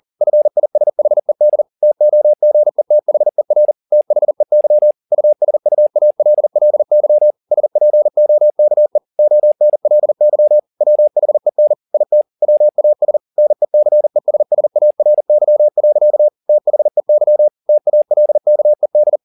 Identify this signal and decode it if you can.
Morse code, heard as a keyed tone, about 40 wpm, 600 Hz. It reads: FISHED TOGETHER THEY USUALLY SPOKE ONLY WHEN IT WAS NECESSARY. THEY TALKED